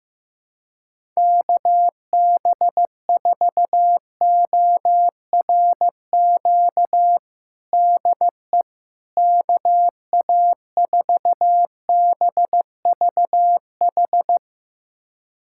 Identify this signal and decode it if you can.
Morse code, heard as a keyed tone, about 15 wpm, 700 Hz. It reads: KB4ORQ DE KA4BVH
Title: USA Callsign DE Callsign - 6 Characters - 15wpm